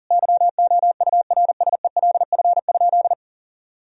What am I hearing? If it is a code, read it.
Morse code, heard as a keyed tone, about 40 wpm, 700 Hz. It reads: YOURSELF?
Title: Sentences from Top 1000 Words - Review Part 1 - 40wpm